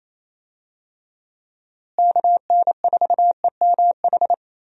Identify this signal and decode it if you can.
Morse code, heard as a keyed tone, about 28 wpm, 700 Hz. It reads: KN4EMH
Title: USA Call Signs - 6 Characters - Repeated 3x - 28wpm